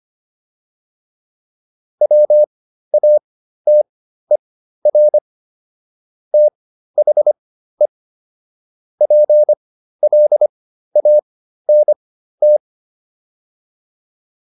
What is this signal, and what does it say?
Morse code, heard as a keyed tone, about 25 wpm, spaced as for 13 wpm, 600 Hz. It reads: WATER THE PLANT
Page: intermediate-practice-files